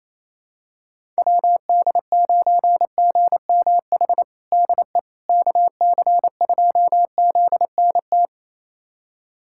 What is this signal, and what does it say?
Morse code, heard as a keyed tone, about 28 wpm, 700 Hz. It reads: WD9GMH DE KC2ZNT
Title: USA Callsign DE Callsign 28wpm